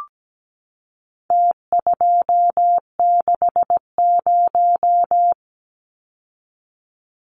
Morse code, heard as a keyed tone, about 17 wpm, 700 Hz. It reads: T260
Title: Numbers - Contest Style - 4-Digit 17wpm